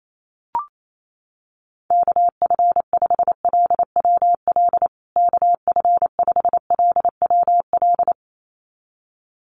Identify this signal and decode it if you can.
Morse code, heard as a keyed tone, about 28 wpm, 700 Hz. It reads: KF5LWL KF5LWL
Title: USA Call Signs Repeated 2x 28wpm